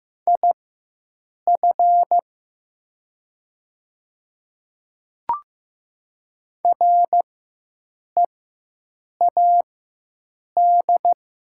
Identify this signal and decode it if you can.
Morse code, heard as a keyed tone, about 15 wpm, spaced as for 7 wpm, 700 Hz. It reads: IF READ
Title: 4x Character Spacing - Top 300 Words - 15wpm